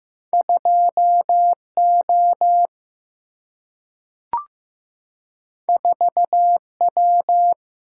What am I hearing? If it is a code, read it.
Morse code, heard as a keyed tone, about 15 wpm, 700 Hz. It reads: 2 O 4 W
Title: Single-Letter Single-Number 15wpm